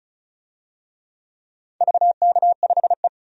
Morse code, heard as a keyed tone, about 36 wpm, 700 Hz. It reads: VK5E
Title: DXCC-100 Call Signs - 4 Characters - 35wpm